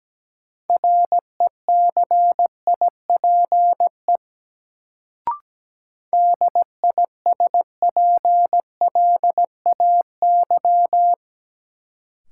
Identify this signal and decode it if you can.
Morse code, heard as a keyed tone, about 17 wpm, 700 Hz. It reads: RECIPE DISPLAY
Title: Sentences from Top 1200 Words - New Word Drill - 17wpm